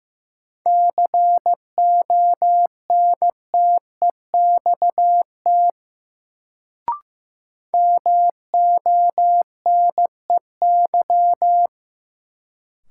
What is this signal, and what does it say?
Morse code, heard as a keyed tone, about 15 wpm, 700 Hz. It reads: CONTEXT MONEY
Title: Sets of 4 Words - New Word Drill - 15wpm